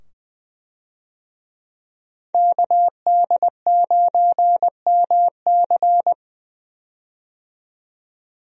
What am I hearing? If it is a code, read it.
Morse code, heard as a keyed tone, about 20 wpm, 700 Hz. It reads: KD9MC
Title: USA Call Signs - 5 Characters - 20wpm